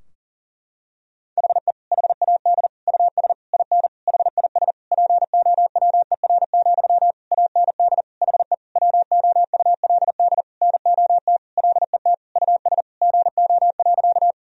Morse code, heard as a keyed tone, about 40 wpm, 700 Hz. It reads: HE HAD US IN HIS POWER, AND HE WOULD NOT LET US GO.
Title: Sentences from Top 1000 Words - Review Part 2 - 40wpm